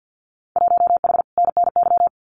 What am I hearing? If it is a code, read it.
Morse code, heard as a keyed tone, about 50 wpm, 700 Hz. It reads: 1H NNY
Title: Winter Field Day Exchanges 50wpm